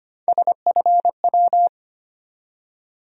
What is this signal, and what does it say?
Morse code, heard as a keyed tone, about 25 wpm, 700 Hz. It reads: SFW
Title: USA Call Sign Postfixes - 3 Letters 25wpm